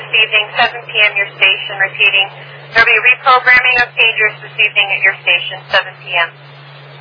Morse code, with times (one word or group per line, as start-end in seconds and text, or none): none